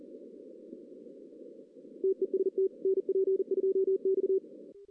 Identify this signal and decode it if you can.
Morse code, heard as a keyed tone, about 40 wpm, 410 Hz. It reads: TEST NP2X
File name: 2016 arrl dx cw / 2016-ARRL-DX-CW-K5TR- - 0-3